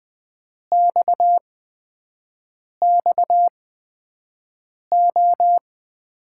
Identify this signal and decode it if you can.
Morse code, heard as a keyed tone, about 20 wpm, spaced as for 5 wpm, 700 Hz. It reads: XXO
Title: USA Call Signs - 5 Characters - 8x Character Spacing - 20wpm